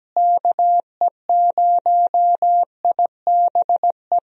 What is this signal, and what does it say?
Morse code, heard as a keyed tone, about 17 wpm, 700 Hz. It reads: KE0IBE
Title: USA Call Signs - 6 Characters - Repeated 3x - 17wpm